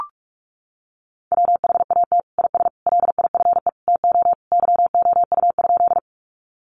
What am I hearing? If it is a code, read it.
Morse code, heard as a keyed tone, about 44 wpm, 700 Hz. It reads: WHAT IS LIFE TO YOU?
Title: Sentences from Top 300 Words 45wpm